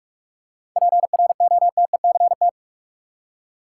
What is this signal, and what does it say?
Morse code, heard as a keyed tone, about 42 wpm, 700 Hz. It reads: PROTECT
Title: Seven Letter Words - 45wpm